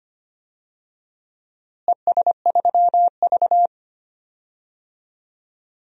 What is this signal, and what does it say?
Morse code, heard as a keyed tone, about 25 wpm, 700 Hz. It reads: ES3V
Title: DXCC-100 Call Signs - 4 Characters - Repeated 2x - 25wpm